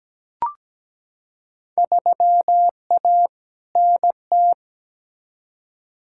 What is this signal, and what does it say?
Morse code, heard as a keyed tone, about 17 wpm, 700 Hz. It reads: 3A NT
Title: Field Day Exchanges Encore - 17wpm